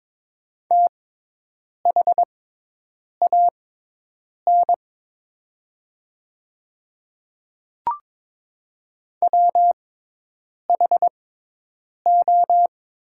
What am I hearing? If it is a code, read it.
Morse code, heard as a keyed tone, about 22 wpm, spaced as for 8 wpm, 700 Hz. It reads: THAN WHO
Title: 6x Character Spacing - Top 100 Words - 22wpm